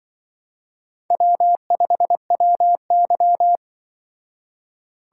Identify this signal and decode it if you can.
Morse code, heard as a keyed tone, about 25 wpm, 700 Hz. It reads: W5WY
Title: Speed Racing - USA 1x2 Call Signs 20wpm